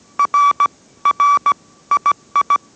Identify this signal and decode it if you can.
Morse code, heard as a keyed tone, about 17 wpm, 1150 Hz. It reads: R R II